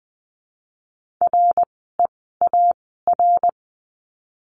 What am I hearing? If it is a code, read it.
Morse code, heard as a keyed tone, about 20 wpm, spaced as for 14 wpm, 700 Hz. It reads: REAR
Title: 054 Lesson 10f - Letter r - Words 20wpm